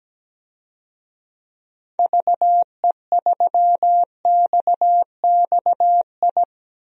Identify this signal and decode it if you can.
Morse code, heard as a keyed tone, about 17 wpm, 700 Hz. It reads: VE3XXI
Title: DXCC-100 Call Signs - 6 Characters - 17wpm